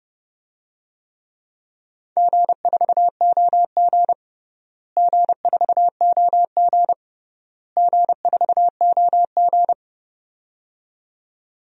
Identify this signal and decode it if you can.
Morse code, heard as a keyed tone, about 30 wpm, 700 Hz. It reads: G4OG G4OG G4OG
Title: DXCC-100 Call Signs - 4 Characters - Repeated 3x - 30wpm